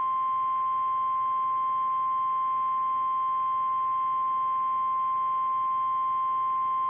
Emergency Alert System (SAME) tones (the 1050 Hz attention tone). It is heard from 0.8 to 5.9 s.
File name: eas / KKPRKXPN / KKPRKXPN